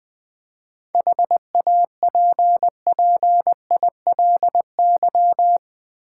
Morse code, heard as a keyed tone, about 21 wpm, 700 Hz. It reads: HAPPILY
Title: 7 Letter Words - First 3 Letters Given - 20wpm